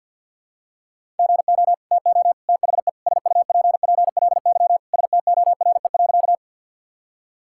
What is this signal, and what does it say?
Morse code, heard as a keyed tone, about 50 wpm, 700 Hz. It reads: GO TO THE SUPPLY STORE.
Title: Sentences from Top 1000 Words - Review Part 5 - 50wpm